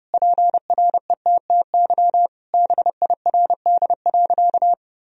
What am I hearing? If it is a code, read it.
Morse code, heard as a keyed tone, about 32 wpm, 700 Hz. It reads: PRETTY BIRD.
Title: Sentences from Top 1000 Words - Review Part 2 - 30wpm